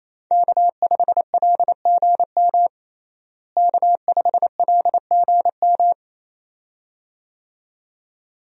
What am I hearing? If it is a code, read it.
Morse code, heard as a keyed tone, about 28 wpm, 700 Hz. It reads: K5LGM K5LGM
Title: USA Call Signs - 5 Characters - Repeated 2x - 28wpm